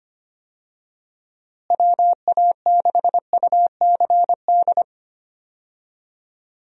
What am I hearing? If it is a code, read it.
Morse code, heard as a keyed tone, about 25 wpm, 700 Hz. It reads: WA6UCD
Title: USA Call Signs - 6 Characters - Repeated 3x - 25wpm